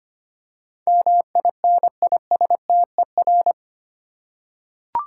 Morse code, heard as a keyed tone, about 25 wpm, 700 Hz. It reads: MINISTER
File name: 8 Letter Words - First 3 Letters Given - 25wpm